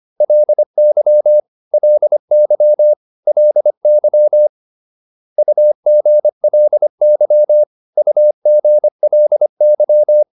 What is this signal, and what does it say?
Morse code, heard as a keyed tone, about 25 wpm, 600 Hz. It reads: LY LY LY UGLY UGLY